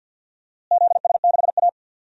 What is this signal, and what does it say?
Morse code, heard as a keyed tone, about 47 wpm, 700 Hz. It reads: ZS6A
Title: DXCC-100 Call Signs - 4 Characters - Repeated 3x - 50wpm